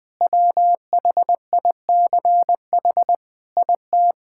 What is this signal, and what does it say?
Morse code, heard as a keyed tone, about 20 wpm, 700 Hz. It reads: WHICH IT
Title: Sentences with 4-Word Phrases - 20wpm